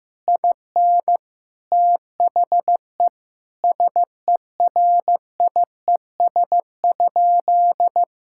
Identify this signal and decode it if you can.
Morse code, heard as a keyed tone, about 15 wpm, 700 Hz. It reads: IN THE SERIES?
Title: Sentences from Top 1000 Words - Review Part 5 - 15wpm